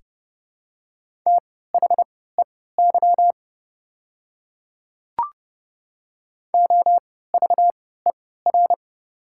Morse code, heard as a keyed tone, about 30 wpm, spaced as for 17 wpm, 700 Hz. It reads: THEY OVER
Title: 3x Character Spacing - Top 100 Words - 30wpm